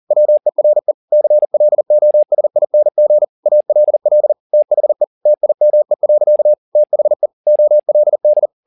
Morse code, heard as a keyed tone, about 40 wpm, 600 Hz. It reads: WERE CLOSING ALL THE TIME. THE OLD
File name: the old man and the sea 2 40wpm